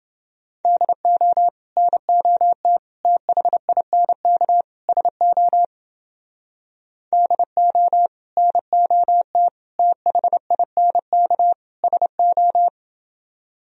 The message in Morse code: DO NOT THINK SO DO NOT THINK SO

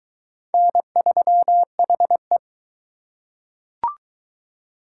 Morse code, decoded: N3HE

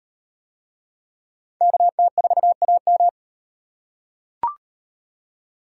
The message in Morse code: KT4AM

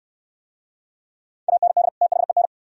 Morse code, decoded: AND THEN

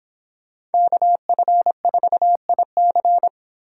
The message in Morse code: KF4IC

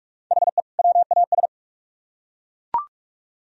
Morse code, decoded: HE WAS

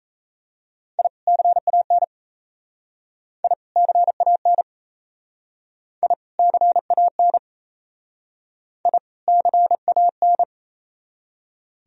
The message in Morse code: I CAN I CAN I CAN I CAN

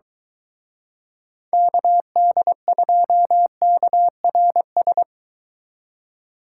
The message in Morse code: KD2KRS

